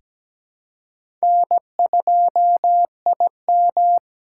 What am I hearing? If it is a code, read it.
Morse code, heard as a keyed tone, about 17 wpm, 700 Hz. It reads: N2IM
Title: USA Call Signs - 4x Character Spacing - 17wpm